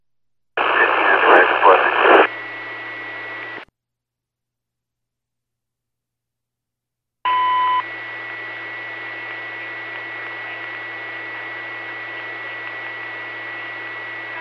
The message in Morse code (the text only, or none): none